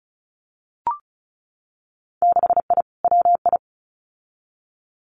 Morse code, heard as a keyed tone, about 36 wpm, 700 Hz. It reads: 6I WI
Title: Winter Field Day Exchanges Encore 35wpm